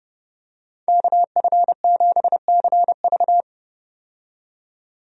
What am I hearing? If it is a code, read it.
Morse code, heard as a keyed tone, about 30 wpm, 700 Hz. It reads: KF7CV
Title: USA Call Signs - 5 Characters - 30wpm